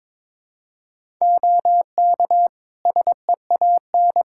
Morse code, heard as a keyed tone, about 22 wpm, 700 Hz. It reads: OK SEAN
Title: Practice QSO - Exchange 4 - Ending - 22wpm